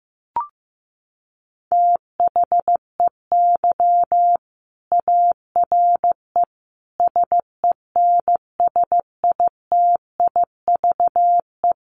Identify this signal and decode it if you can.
Morse code, heard as a keyed tone, about 14 wpm, 700 Hz. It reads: THEY ARE SENSITIVE
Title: Sentences from Top 1400 Words 15wpm